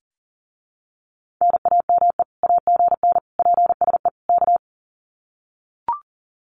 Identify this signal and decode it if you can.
Morse code, heard as a keyed tone, about 40 wpm, 700 Hz. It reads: NAME AGN PSE K
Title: Common QSO Phrases Encore 40wpm